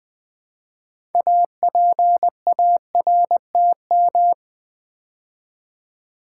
Morse code, heard as a keyed tone, about 20 wpm, 700 Hz. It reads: APARTM
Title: Build A Word - 3 Letters At A Time - 9-Letter Words - 20wpm